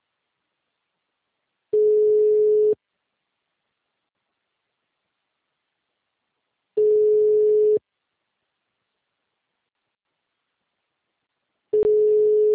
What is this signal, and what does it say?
Ringback tone (Europe, 425 Hz).
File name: out- -2002- - - .17216